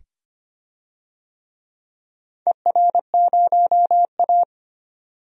Morse code, heard as a keyed tone, about 25 wpm, 700 Hz. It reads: ER0A
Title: DXCC-100 Call Signs - 4 Characters - 25wpm